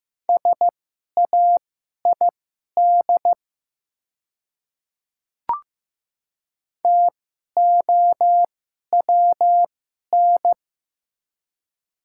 Morse code, heard as a keyed tone, about 15 wpm, spaced as for 11 wpm, 690 Hz. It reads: SAID TOWN
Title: 2x Character Spacing - Top 500 Words - 15wpm